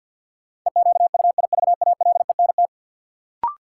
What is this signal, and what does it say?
Morse code, heard as a keyed tone, about 52 wpm, 700 Hz. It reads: EQUIVALENT
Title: Sentences from Top 1600 Words - New Word Drill Alt - 50wpm